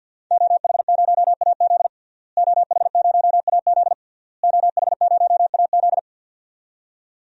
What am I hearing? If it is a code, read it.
Morse code, heard as a keyed tone, about 51 wpm, 700 Hz. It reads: OH0AZ OH0AZ OH0AZ
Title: DXCC-100 Call Signs - 5 Characters - Repeated 3x - 50wpm